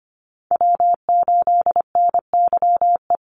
Morse code, heard as a keyed tone, about 25 wpm, 700 Hz. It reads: W8NYE
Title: USA Call Signs 25wpm